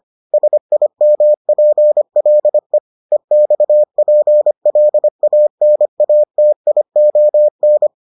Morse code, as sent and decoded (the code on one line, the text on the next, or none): ... .. -- .--. .-.. / . -..- .--. .-.. .- -. .- - .. --- -.
SIMPLE EXPLANATION